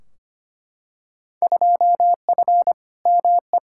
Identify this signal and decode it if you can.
Morse code, heard as a keyed tone, about 25 wpm, 700 Hz. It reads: 2F ME
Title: Field Day Exchanges - 25wpm